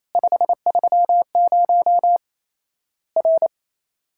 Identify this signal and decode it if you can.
Morse code, heard as a keyed tone, about 28 wpm, 700 Hz. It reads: 530 R
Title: Worked All Europe DX Contest - QTC Exchange - 28wpm